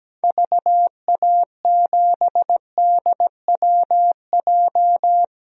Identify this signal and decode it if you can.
Morse code, heard as a keyed tone, about 17 wpm, 700 Hz. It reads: VA7DWJ
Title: DXCC-100 Call Signs - 6 Characters - 17wpm